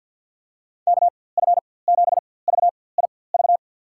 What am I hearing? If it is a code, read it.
Morse code, heard as a keyed tone, about 51 wpm, spaced as for 23 wpm, 700 Hz. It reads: KF7VIV
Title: USA Call Signs - 6 Characters - 4x Character Spacing - 50wpm